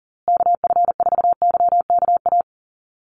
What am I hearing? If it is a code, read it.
Morse code, heard as a keyed tone, about 40 wpm, 700 Hz. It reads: KF4YKA
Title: USA Call Signs Repeated 2x 40wpm